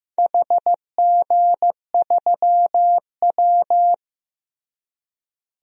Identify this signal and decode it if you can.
Morse code, heard as a keyed tone, about 15 wpm, 700 Hz. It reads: HG3W
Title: DXCC-100 Call Signs - 4 Characters - Repeated 3x - 15wpm